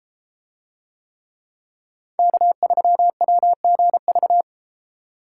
Morse code, heard as a keyed tone, about 32 wpm, 700 Hz. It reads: K3WGV